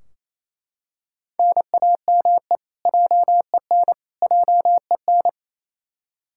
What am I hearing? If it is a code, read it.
Morse code, heard as a keyed tone, about 28 wpm, 700 Hz. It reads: NAME JEN JEN
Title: Practice QSO - Exchange 1 - Beginning - 28wpm